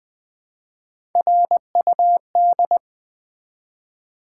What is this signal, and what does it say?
Morse code, heard as a keyed tone, about 20 wpm, 700 Hz. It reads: RUD
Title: Build A Word - 1 Letter At A Time - 4-Letter Words - 20wpm